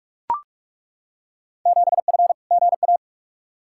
Morse code, heard as a keyed tone, about 42 wpm, 700 Hz. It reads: 7F GA
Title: Field Day Exchanges Encore - 45wpm